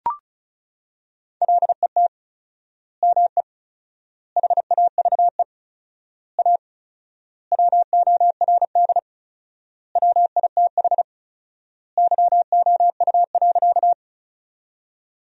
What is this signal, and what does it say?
Morse code, heard as a keyed tone, about 34 wpm, 700 Hz. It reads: LET ME HAVE A WORD WITH YOU.
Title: 4x Word-Spacing - Sentences from Top 300 Words - 35wpm